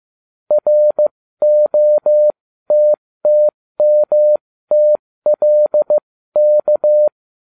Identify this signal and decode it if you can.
Morse code, heard as a keyed tone, about 15 wpm, spaced as for 13 wpm, 600 Hz. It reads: ROTTMTLK